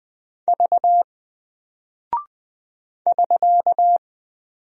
Morse code, heard as a keyed tone, about 20 wpm, 700 Hz. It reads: V <SK>
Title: Single Character (Letter:Number:Punctuation:Prosign) Alt 20wpm